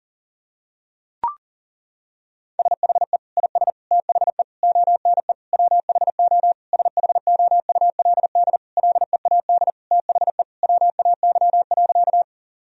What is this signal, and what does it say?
Morse code, heard as a keyed tone, about 37 wpm, 700 Hz. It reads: SHE IS THE ONE WHO SHOULD LEAD THE WAY.
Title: Sentences from Top 1000 Words - Review Part 2 - 40wpm